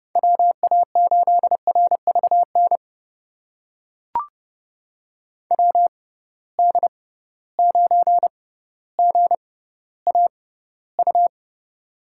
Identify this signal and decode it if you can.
Morse code, heard as a keyed tone, about 30 wpm, spaced as for 15 wpm, 700 Hz. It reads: WA8RVN WD9GAU
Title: USA Call Signs - 6x Character Spacing - 30wpm